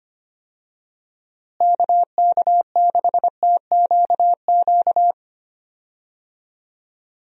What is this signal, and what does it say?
Morse code, heard as a keyed tone, about 25 wpm, 700 Hz. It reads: KK6TQQ
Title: USA Call Signs - 6 Characters - Repeated 3x - 25wpm